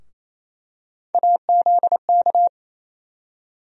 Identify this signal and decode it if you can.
Morse code, heard as a keyed tone, about 28 wpm, 700 Hz. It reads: AZK
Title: USA Call Sign Postfixes - 3 Letters 28wpm